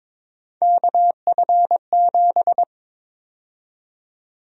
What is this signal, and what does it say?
Morse code, heard as a keyed tone, about 22 wpm, 700 Hz. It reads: KF7
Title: USA Call Sign Prefixes - 22wpm